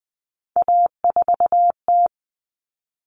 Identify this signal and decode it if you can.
Morse code, heard as a keyed tone, about 20 wpm, 700 Hz. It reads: A4T
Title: All Cut Numbers - 3-Digit 20wpm